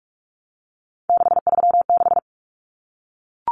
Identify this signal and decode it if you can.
Morse code, heard as a keyed tone, about 44 wpm, 700 Hz. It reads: 636
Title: Common Cut Numbers - 3-Digit 45wpm